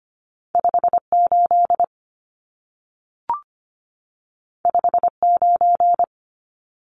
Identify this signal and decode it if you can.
Morse code, heard as a keyed tone, about 25 wpm, 700 Hz. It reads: 58 59
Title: Numbers - 2-Digit 25wpm